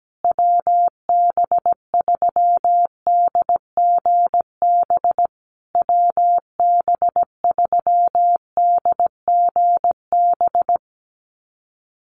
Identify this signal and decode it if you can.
Morse code, heard as a keyed tone, about 17 wpm, 700 Hz. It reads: WB3DGB WB3DGB
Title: USA Call Signs Repeated 3x 17wpm